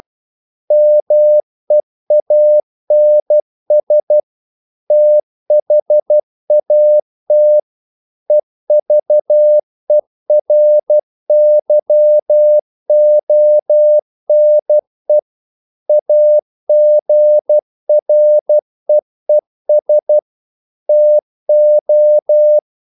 Morse code, heard as a keyed tone, about 12 wpm, 600 Hz. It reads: MEANS THAT EVERYONE AGREES TO
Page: English quotations in CW (telegraphy - morse)